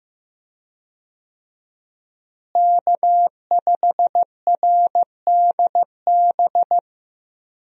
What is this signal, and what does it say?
Morse code, heard as a keyed tone, about 15 wpm, 700 Hz. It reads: K5RDB
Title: USA Call Signs - 5 Characters - Repeated 3x - 15wpm